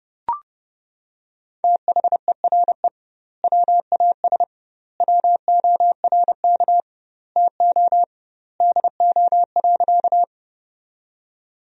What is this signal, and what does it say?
Morse code, heard as a keyed tone, about 31 wpm, 700 Hz. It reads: THERE WAS WORK TO DO.
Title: 2x Word-Spacing - Sentences from Top 200 Words - 30wpm